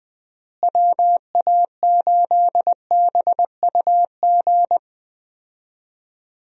Morse code, heard as a keyed tone, about 20 wpm, 700 Hz. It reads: WA8BUG
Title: USA Call Signs - 6 Characters - Repeated 2x - 20wpm